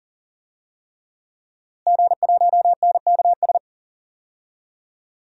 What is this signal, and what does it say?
Morse code, heard as a keyed tone, about 40 wpm, 700 Hz. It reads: G1NKS
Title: DXCC-100 Call Signs - 5 Characters - Repeated 3x - 40wpm